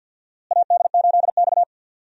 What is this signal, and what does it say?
Morse code, heard as a keyed tone, about 52 wpm, 700 Hz. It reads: AD8X
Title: USA Call Signs - 4 Characters - Repeated 3x - 50wpm